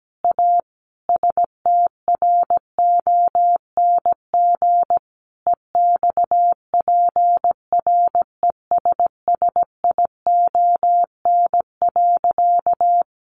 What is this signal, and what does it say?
Morse code, heard as a keyed tone, about 17 wpm, 700 Hz. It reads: A STRONG EXPRESSION.
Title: Sentences from Top 1200 Words 17wpm